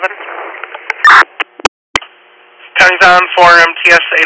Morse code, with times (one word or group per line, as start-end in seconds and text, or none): none